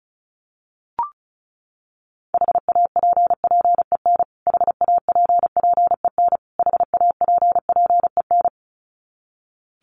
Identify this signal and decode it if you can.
Morse code, heard as a keyed tone, about 34 wpm, 700 Hz. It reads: HAPPEN HAPPEN HAPPEN
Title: Top 500 Words Repeated 35wpm